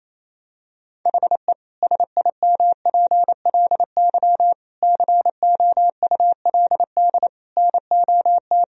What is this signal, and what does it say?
Morse code, heard as a keyed tone, about 28 wpm, 700 Hz. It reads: HE SIMPLY COULD NOT
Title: Sentences from Top 1000 Words - Review Part 4 - 28wpm